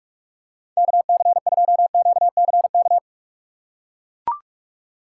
Morse code, heard as a keyed tone, about 43 wpm, 700 Hz. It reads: KC2QCK
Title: USA Call Signs - 2x Character Spacing - 45wpm